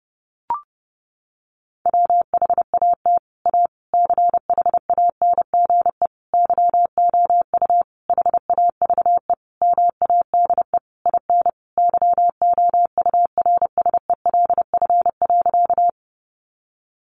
Morse code, heard as a keyed tone, about 28 wpm, 700 Hz. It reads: WHAT A CHANGE YOU HAVE MADE IN YOURSELF.
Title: Sentences from Top 600 Words 30wpm